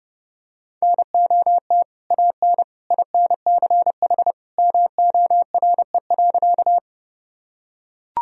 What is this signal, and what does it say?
Morse code, heard as a keyed tone, about 32 wpm, 700 Hz. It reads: NOT AN INCH MORE.
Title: Sentences from Top 1000 Words - Review Part 3 - 30wpm